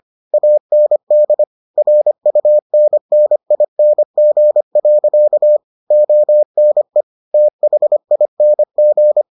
Morse code, heard as a keyed tone, about 25 wpm, 600 Hz. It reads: AND RUNNING. ONE THING